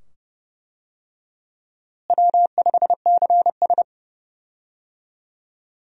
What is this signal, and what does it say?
Morse code, heard as a keyed tone, about 29 wpm, 700 Hz. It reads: W5CS
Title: USA Call Signs - 4 Characters - Repeated 2x - 30wpm